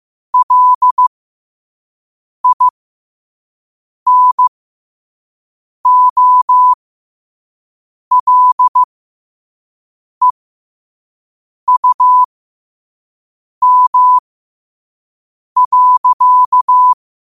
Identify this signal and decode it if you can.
Morse code, heard as a keyed tone, about 15 wpm, spaced as for 5 wpm, 1000 Hz. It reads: LINOLEUM.
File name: 5wpm